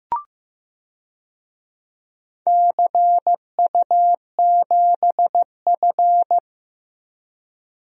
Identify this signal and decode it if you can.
Morse code, heard as a keyed tone, about 15 wpm, 700 Hz. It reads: CU7F
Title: DXCC-100 Call Signs - 4 Characters - Repeated 2x - 15wpm